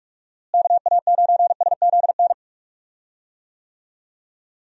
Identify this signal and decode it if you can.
Morse code, heard as a keyed tone, about 46 wpm, 700 Hz. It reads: KA9SZN
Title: USA Call Signs - 6 Characters - Repeated 2x - 45wpm